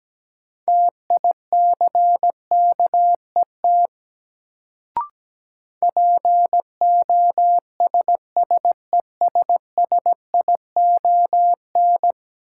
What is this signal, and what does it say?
Morse code, heard as a keyed tone, about 17 wpm, 700 Hz. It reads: TICKET POSSESSION
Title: Sentences from Top 1400 Words - New Word Drill Alt - 17wpm